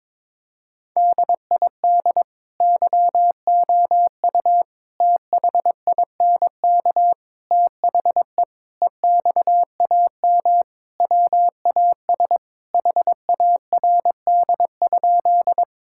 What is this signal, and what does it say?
Morse code, heard as a keyed tone, about 22 wpm, 700 Hz. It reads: DID YOU THINK THE EXAM WAS HARD?
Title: Sentences from Top 1000 Words - Review Part 5 - 22wpm